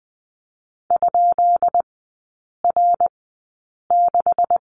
Morse code, heard as a keyed tone, about 20 wpm, spaced as for 8 wpm, 700 Hz. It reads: ?R6
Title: 133 Lesson 22e - Question Mark - Three Characters 20wpm